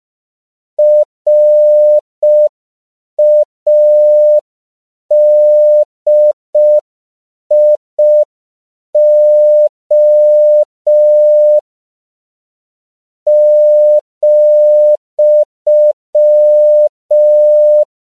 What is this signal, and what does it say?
Morse code, heard as a keyed tone, about 5 wpm, 600 Hz. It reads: RADIO ,
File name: CW / 5PPM